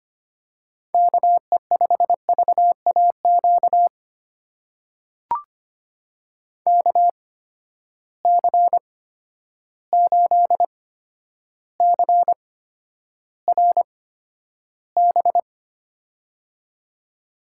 Morse code, heard as a keyed tone, about 25 wpm, spaced as for 11 wpm, 700 Hz. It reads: KE5VAQ KC8CRB
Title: USA Call Signs - 8x Character Spacing - 25wpm